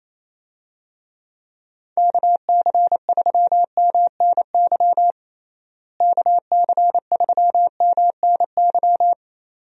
Morse code, heard as a keyed tone, about 27 wpm, 700 Hz. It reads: KC3MNY KC3MNY
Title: USA Call Signs - 6 Characters - Repeated 2x - 28wpm